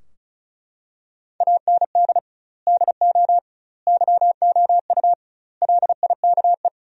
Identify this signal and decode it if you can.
Morse code, heard as a keyed tone, about 34 wpm, 700 Hz. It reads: AND DO YOU LIKE
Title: 2x Word-Spacing - Sentences from Top 100 Words - 35wpm